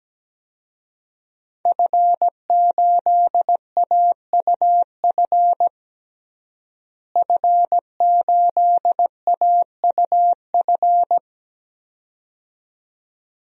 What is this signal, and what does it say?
Morse code, heard as a keyed tone, about 17 wpm, 700 Hz. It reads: F8AUF F8AUF
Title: DXCC-100 Call Signs - 5 Characters - Repeated 2x - 17wpm